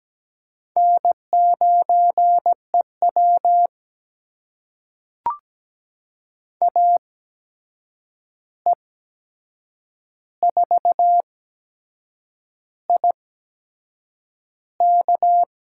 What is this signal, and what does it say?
Morse code, heard as a keyed tone, about 17 wpm, spaced as for 7 wpm, 700 Hz. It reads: N9EW AE4IK
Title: USA Call Signs - 8x Character Spacing - 17wpm